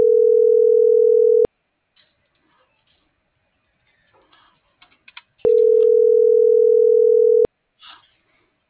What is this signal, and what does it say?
Ringback tone (North America), 440 + 480 Hz.